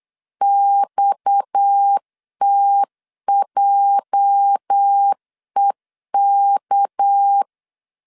Morse code, heard as a keyed tone, about 8 wpm, 800 Hz. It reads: XTJEK
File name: LEZIONI / 10 SIGNS / 10A SLASH 8wpm 3452